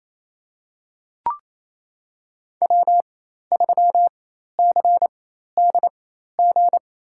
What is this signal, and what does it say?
Morse code, heard as a keyed tone, about 28 wpm, spaced as for 13 wpm, 700 Hz. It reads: W3CDG
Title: USA Call Signs - 5 Characters - 4x Character Spacing - 28wpm